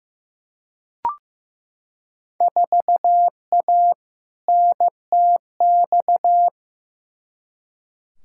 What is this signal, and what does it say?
Morse code, heard as a keyed tone, about 15 wpm, 700 Hz. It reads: 4A NTX
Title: Field Day Exchanges - 15wpm